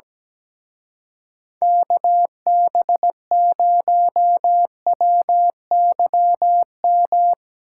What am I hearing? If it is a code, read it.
Morse code, heard as a keyed tone, about 19 wpm, 700 Hz. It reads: KB0WYM
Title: Speed Racing - USA 2x3 Call Signs 17wpm